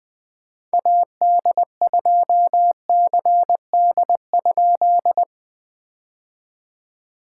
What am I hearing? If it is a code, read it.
Morse code, heard as a keyed tone, about 19 wpm, 700 Hz. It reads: AD2CD?
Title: POTA QSO Practice - 20wpm